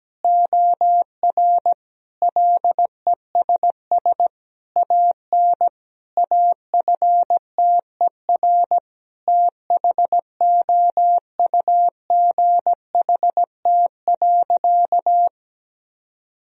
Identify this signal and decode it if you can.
Morse code, heard as a keyed tone, about 17 wpm, 700 Hz. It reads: OR LESS AN AFTER THOUGHT.
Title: Sentences from Top 1000 Words - Review Part 2 - 17wpm